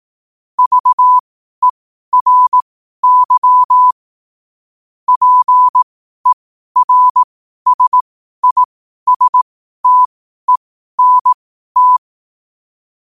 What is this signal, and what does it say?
Morse code, heard as a keyed tone, about 18 wpm, spaced as for 12 wpm, 1000 Hz. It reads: VERY PERSISTENT
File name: qotdfiles / 12wpm